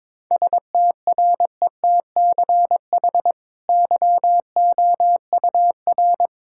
Morse code, heard as a keyed tone, about 22 wpm, 700 Hz. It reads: STRETCH YOUR
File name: practice / 22wpm / Easy Sentences with Hints - 22wpm